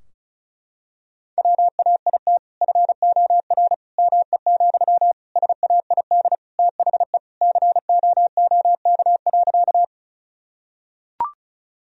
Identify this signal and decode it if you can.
Morse code, heard as a keyed tone, about 34 wpm, 700 Hz. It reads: WAIT FOR ME, SAID THE COOK.
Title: Sentences from Top 1000 Words - Review Part 4 - 35wpm